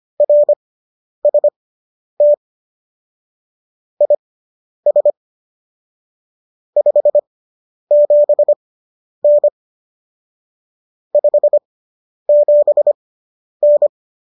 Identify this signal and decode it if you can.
Morse code, heard as a keyed tone, about 25 wpm, spaced as for 10 wpm, 590 Hz. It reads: RST IS 57N 57N